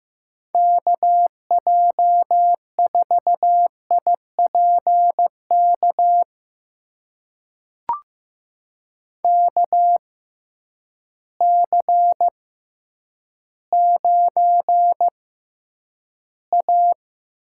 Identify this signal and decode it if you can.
Morse code, heard as a keyed tone, about 15 wpm, spaced as for 9 wpm, 700 Hz. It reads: KJ4IPK KC9A
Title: USA Call Signs - 6 Characters - 6x Character Spacing - 15wpm